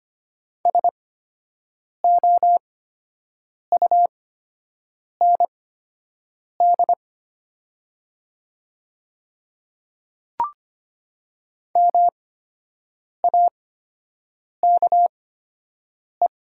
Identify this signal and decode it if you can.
Morse code, heard as a keyed tone, about 25 wpm, spaced as for 7 wpm, 700 Hz. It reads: SOUND MAKE
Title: 8x Character Spacing - Top 100 Words - 25wpm